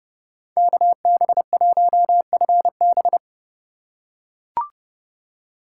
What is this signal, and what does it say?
Morse code, heard as a keyed tone, about 32 wpm, 700 Hz. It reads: KB1FB
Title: USA Call Signs - 5 Characters - 4x Character Spacing - 30wpm